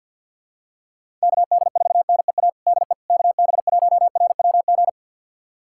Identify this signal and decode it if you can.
Morse code, heard as a keyed tone, about 50 wpm, 700 Hz. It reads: KD4NEA DE KB1RWG
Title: USA Callsign DE Callsign - 6 Characters - 50wpm